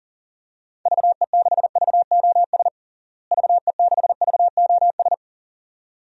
Morse code, heard as a keyed tone, about 37 wpm, 700 Hz. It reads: VE6VOS VE6VOS
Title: DXCC-100 Call Signs - 6 Characters - Repeated 2x - 40wpm